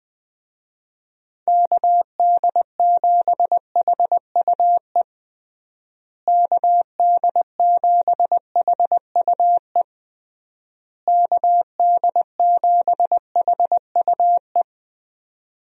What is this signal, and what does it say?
Morse code, heard as a keyed tone, about 20 wpm, 700 Hz. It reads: KD7HUE KD7HUE KD7HUE
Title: USA Call Signs - 6 Characters - Repeated 3x - 20wpm